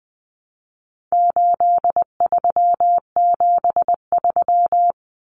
Morse code, heard as a keyed tone, about 20 wpm, 700 Hz. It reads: 8373
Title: Numbers - 4-Digit 20wpm